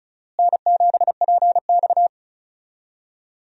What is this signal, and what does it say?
Morse code, heard as a keyed tone, about 36 wpm, 700 Hz. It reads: N7PX
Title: USA Call Signs - 4 Characters - Repeated 2x - 35wpm